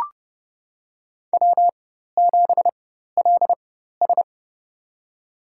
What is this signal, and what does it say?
Morse code, heard as a keyed tone, about 30 wpm, spaced as for 14 wpm, 700 Hz. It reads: W7LS
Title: USA Call Signs - 4x Character Spacing - 30wpm